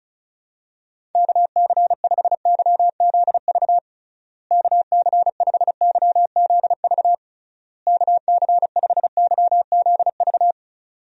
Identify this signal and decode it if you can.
Morse code, heard as a keyed tone, about 34 wpm, 700 Hz. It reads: KC5YZV KC5YZV KC5YZV
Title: USA Call Signs - 6 Characters - Repeated 3x - 35wpm